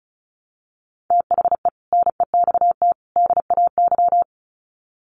Morse code, heard as a keyed tone, about 34 wpm, 700 Hz. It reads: THE NEXT DAY
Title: Sets of 3 Words 35wpm